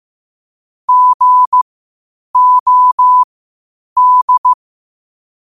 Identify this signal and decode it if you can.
Morse code, heard as a keyed tone, about 15 wpm, spaced as for 8 wpm, 1000 Hz. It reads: GOD
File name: qotdfiles / 8wpm